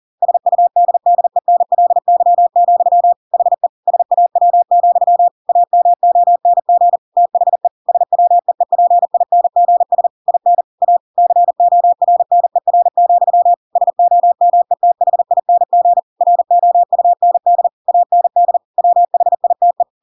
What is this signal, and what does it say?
Morse code, heard as a keyed tone, about 40 wpm, 700 Hz. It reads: SUDDENLY, HE SAW, AMONG THE SWEEPINGS IN A CORNER, SOMETHING ROUND AND WHITE
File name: Pinocchio / Pinocchio ch5 40 40 ik1hge